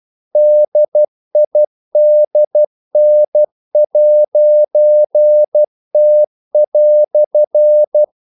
Morse code, heard as a keyed tone, about 12 wpm, 600 Hz. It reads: DIDN'T"
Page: English quotations in CW (telegraphy - morse)